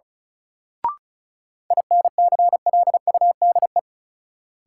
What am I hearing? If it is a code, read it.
Morse code, heard as a keyed tone, about 34 wpm, 700 Hz. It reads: INCLUDE
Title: Sentences from Top 1000 Words - Review Part 3 - New Word Drill - 35wpm